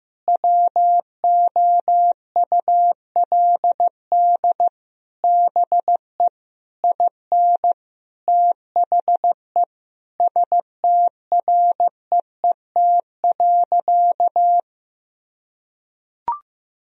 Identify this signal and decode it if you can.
Morse code, heard as a keyed tone, about 15 wpm, 700 Hz. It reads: WOULD BE IN THE STREET.
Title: Sentences from Top 1000 Words - Review Part 3 - 15wpm